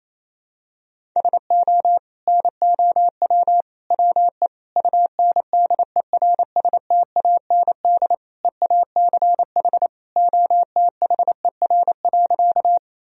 Morse code, heard as a keyed tone, about 28 wpm, 700 Hz. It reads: SO NOW WE UNDERSTAND EACH OTHER.
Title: Sentences from Top 1000 Words - Review Part 1 - 28wpm